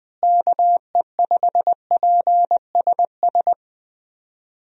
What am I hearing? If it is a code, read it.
Morse code, heard as a keyed tone, about 20 wpm, 700 Hz. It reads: KE5PSS
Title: USA Call Signs - 6 Characters - Repeated 3x - 20wpm